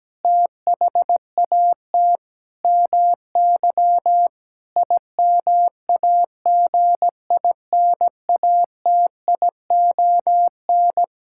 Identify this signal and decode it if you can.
Morse code, heard as a keyed tone, about 17 wpm, 700 Hz. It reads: THAT MY IMAGINATION
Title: Sentences from Top 1600 Words 17wpm